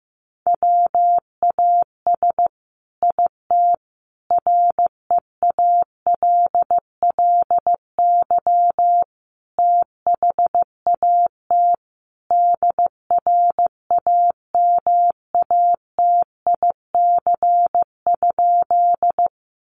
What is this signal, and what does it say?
Morse code, heard as a keyed tone, about 15 wpm, 700 Hz. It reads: WAS IT REALLY THAT DRAMATIC?
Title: Sentences from Top 1300 Words 15wpm